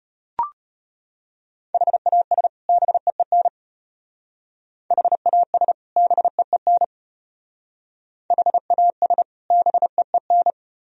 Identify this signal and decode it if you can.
Morse code, heard as a keyed tone, about 30 wpm, 700 Hz. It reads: HAS BEEN HAS BEEN HAS BEEN
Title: Speed-Racing - Sets of 2 Words 25wpm